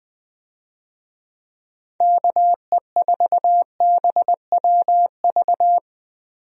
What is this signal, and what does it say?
Morse code, heard as a keyed tone, about 21 wpm, 700 Hz. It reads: KE4BWV
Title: USA Call Signs - 6 Characters - 20wpm